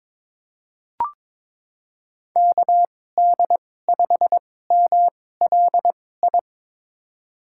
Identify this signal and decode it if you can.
Morse code, heard as a keyed tone, about 22 wpm, spaced as for 16 wpm, 700 Hz. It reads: KD5MLI
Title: USA Call Signs - 6 Characters - 2x Character Spacing - 22wpm